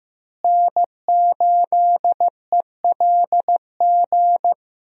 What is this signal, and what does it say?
Morse code, heard as a keyed tone, about 15 wpm, 700 Hz. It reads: N8ELG
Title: USA Call Signs - 5 Characters - Repeated 3x - 15wpm